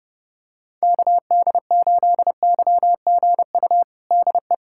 Morse code, heard as a keyed tone, about 32 wpm, 700 Hz. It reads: KD8YGU DE
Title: USA Callsign DE Callsign - 6 Characters - 30wpm